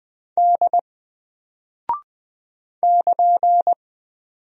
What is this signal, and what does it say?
Morse code, heard as a keyed tone, about 20 wpm, 700 Hz. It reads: D <KN>
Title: Single Character (Letter:Number:Punctuation:Prosign) 20wpm